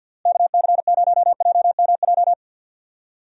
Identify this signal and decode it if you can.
Morse code, heard as a keyed tone, about 51 wpm, 700 Hz. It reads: KC0JMJ
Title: USA Call Signs - 6 Characters - Repeated 3x - 50wpm